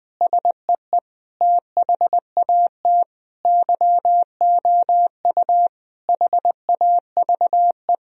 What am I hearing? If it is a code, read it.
Morse code, heard as a keyed tone, about 20 wpm, 700 Hz. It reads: SEE THAT YOU HAVE
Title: Sentences from Top 1000 Words - Review Part 3 - 20wpm